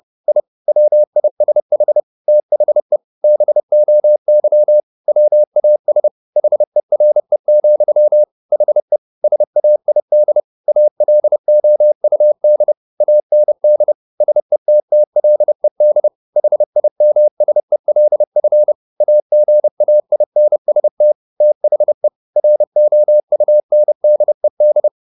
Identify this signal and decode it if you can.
Morse code, heard as a keyed tone, about 30 wpm, 600 Hz. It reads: I WISH THE BOY WAS HERE, HE SAID ALOUD AND SETTLED HIMSELF AGAINST THE ROUNDED
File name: the old man and the sea / the old man and the sea 1 30wpm